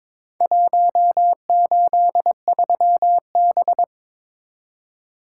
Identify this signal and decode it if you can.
Morse code, heard as a keyed tone, about 23 wpm, 700 Hz. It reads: 183B
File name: practice / 22wpm / Four Letter-Number Combinations - 22wpm